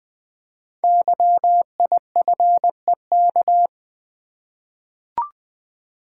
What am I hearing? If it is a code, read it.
Morse code, heard as a keyed tone, about 20 wpm, 700 Hz. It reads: YIFEK
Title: Code Groups - 5 Letters - 20wpm